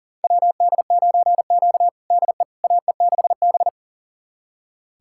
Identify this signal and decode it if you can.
Morse code, heard as a keyed tone, about 40 wpm, 700 Hz. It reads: WD9Q DE AE6B
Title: USA Callsign DE Callsign - 4 Characters - 40wpm